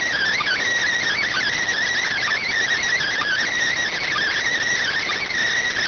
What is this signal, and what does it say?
Emergency Alert System (SAME) tones, heard as harsh buzzing bursts. It is heard from 1.2 to 5.2 s.